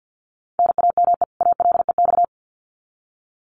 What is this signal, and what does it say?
Morse code, heard as a keyed tone, about 50 wpm, 700 Hz. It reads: NAME ALEX
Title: Common QSO Phrases 50wpm